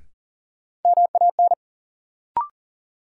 Morse code, heard as a keyed tone, about 40 wpm, 700 Hz. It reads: MAN
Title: Sentences from Top 200 Words - New Word Drill - 40wpm